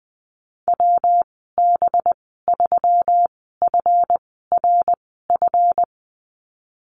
Morse code, heard as a keyed tone, about 20 wpm, spaced as for 15 wpm, 700 Hz. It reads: WB3FRF
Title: 200 Lesson 32g - Letter k - Call Signs 20wpm